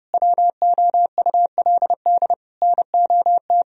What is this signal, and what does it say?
Morse code, heard as a keyed tone, about 32 wpm, 700 Hz. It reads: WOULD NOT
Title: Sentences from Top 1000 Words - Review Part 4 - 30wpm